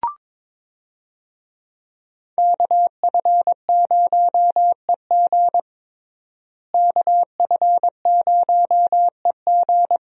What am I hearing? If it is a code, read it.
Morse code, heard as a keyed tone, about 22 wpm, 700 Hz. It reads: KF0EG KF0EG
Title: USA Call Signs - 5 Characters - Repeated 2x - 22wpm